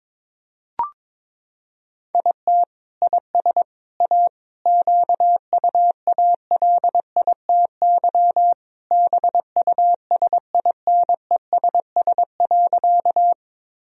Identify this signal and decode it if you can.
Morse code, heard as a keyed tone, about 22 wpm, 700 Hz. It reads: IT IS A QUALITY BUSINESS.
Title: Sentences from Top 1000 Words - Review Part 4 - 22wpm